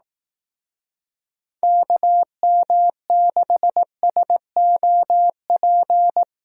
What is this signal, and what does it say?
Morse code, heard as a keyed tone, about 18 wpm, 700 Hz. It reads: KM6SOP
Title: Speed Racing - USA 2x3 Call Signs 15wpm